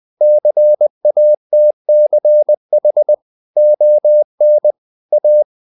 Morse code, heard as a keyed tone, about 20 wpm, 600 Hz. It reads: CATCH ON A
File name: the old man and the sea / the old man and the sea part 1 20wpm